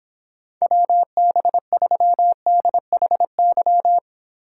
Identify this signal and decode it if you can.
Morse code, heard as a keyed tone, about 26 wpm, 700 Hz. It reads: WB3DHY